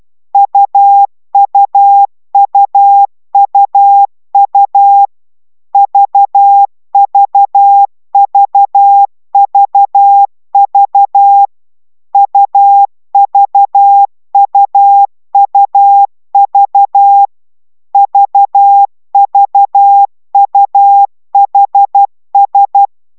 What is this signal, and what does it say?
Morse code, heard as a keyed tone, about 12 wpm, 800 Hz. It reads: UUUUU VVVVV UVUUV VVUHS
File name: taviro tanulas / cw 69 04 10